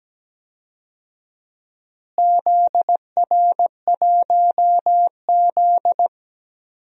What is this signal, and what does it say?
Morse code, heard as a keyed tone, about 17 wpm, 700 Hz. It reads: ZR1Z
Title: DXCC-100 Call Signs - 4 Characters - Repeated 3x - 17wpm